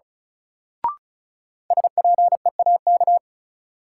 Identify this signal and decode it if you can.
Morse code, heard as a keyed tone, about 36 wpm, 700 Hz. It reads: SPEAK
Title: Sentences from Top 1100 Words - New Word Drill Alt - 35wpm